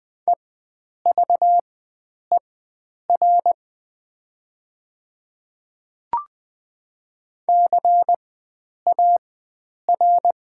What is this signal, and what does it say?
Morse code, heard as a keyed tone, about 20 wpm, spaced as for 9 wpm, 700 Hz. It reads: EVER CAR
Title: 4x Character Spacing - Top 500 Words - 20wpm